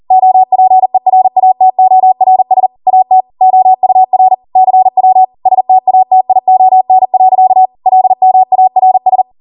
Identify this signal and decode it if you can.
Morse code, heard as a keyed tone, about 40 wpm, 750 Hz. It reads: OPERATORS AT OUR CW STATION. LMARS